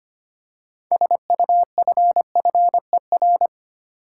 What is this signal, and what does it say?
Morse code, heard as a keyed tone, about 25 wpm, 700 Hz. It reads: SUFFER
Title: Six Letter Words - 25wpm